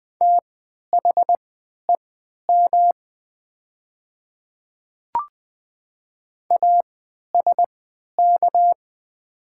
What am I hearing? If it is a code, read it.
Morse code, heard as a keyed tone, about 20 wpm, spaced as for 11 wpm, 700 Hz. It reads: THEM ASK
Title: 3x Character Spacing - Top 200 Words - 20wpm